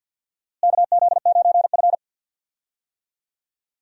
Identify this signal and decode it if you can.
Morse code, heard as a keyed tone, about 46 wpm, 700 Hz. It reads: KZ9F
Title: USA Call Signs - 4 Characters - Repeated 2x - 50wpm